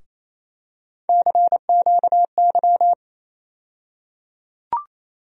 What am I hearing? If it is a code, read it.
Morse code, heard as a keyed tone, about 28 wpm, 700 Hz. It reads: CQY
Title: USA Call Sign Postfixes - 3 Letters 28wpm